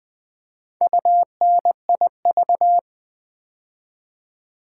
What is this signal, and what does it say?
Morse code, heard as a keyed tone, about 20 wpm, 700 Hz. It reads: UNIV
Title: Build A Word - 2 Letters At A Time - 8-Letter Words - 20wpm